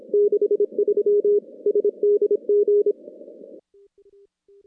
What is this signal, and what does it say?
Morse code, heard as a keyed tone, about 26 wpm, 410 Hz. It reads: 63 SDG
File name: contests / CW-SS-k5tr- -2100